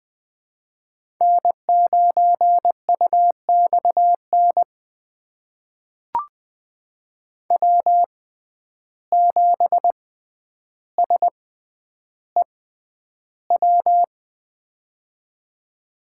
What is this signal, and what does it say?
Morse code, heard as a keyed tone, about 20 wpm, spaced as for 10 wpm, 700 Hz. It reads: N9UXN W7SEW
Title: USA Call Signs - 5 Characters - 6x Character Spacing - 20wpm